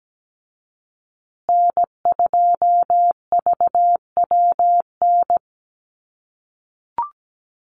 Morse code, heard as a keyed tone, about 17 wpm, 700 Hz. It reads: N2VWN
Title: USA Call Signs Repeated 3x 17wpm